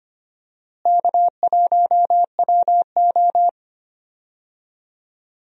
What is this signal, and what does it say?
Morse code, heard as a keyed tone, about 25 wpm, 700 Hz. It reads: K1WO
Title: USA Call Signs - 4 Characters - Repeated 2x - 25wpm